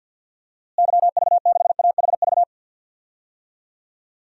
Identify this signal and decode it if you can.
Morse code, heard as a keyed tone, about 53 wpm, 700 Hz. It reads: YV6AHV
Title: DXCC-100 Call Signs - 6 Characters - Repeated 2x - 50wpm